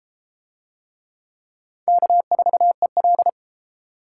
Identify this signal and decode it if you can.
Morse code, heard as a keyed tone, about 30 wpm, 700 Hz. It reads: K4EL